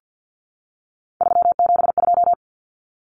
Morse code, heard as a keyed tone, about 50 wpm, 700 Hz. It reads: 372
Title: Common Cut Numbers - 3-Digit 50wpm